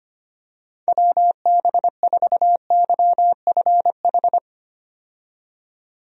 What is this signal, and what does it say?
Morse code, heard as a keyed tone, about 25 wpm, 700 Hz. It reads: WB4YFH
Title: USA Call Signs - 6 Characters - 25wpm